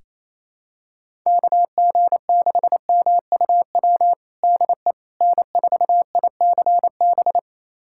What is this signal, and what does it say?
Morse code, heard as a keyed tone, about 28 wpm, 700 Hz. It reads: KG6MUW DE N4ICB
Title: USA Callsign DE Callsign 28wpm